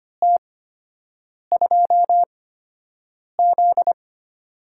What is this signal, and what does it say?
Morse code, heard as a keyed tone, about 25 wpm, spaced as for 7 wpm, 700 Hz. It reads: T2Z
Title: USA Call Signs - 4 Characters - 8x Character Spacing - 25wpm